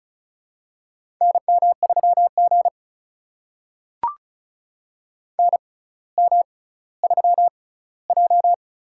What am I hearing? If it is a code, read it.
Morse code, heard as a keyed tone, about 36 wpm, spaced as for 18 wpm, 700 Hz. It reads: NM3G NM3J
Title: USA Call Signs - 4 Characters - 6x Character Spacing - 35wpm